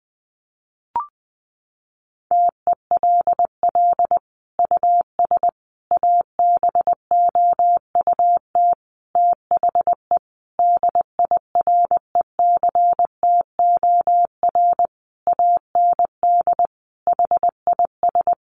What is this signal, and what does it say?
Morse code, heard as a keyed tone, about 19 wpm, 700 Hz. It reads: TELL US ABOUT THE DIRECTOR AND HIS
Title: Sentences from Top 1300 Words 20wpm